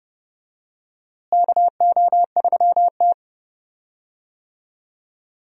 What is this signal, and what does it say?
Morse code, heard as a keyed tone, about 30 wpm, 700 Hz. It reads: KO3T
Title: USA Call Signs - 4 Characters - Repeated 2x - 30wpm